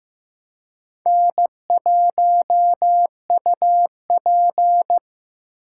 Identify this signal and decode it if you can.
Morse code, heard as a keyed tone, about 15 wpm, 700 Hz. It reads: N1UP
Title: USA Call Signs - 3x Character Spacing - 15wpm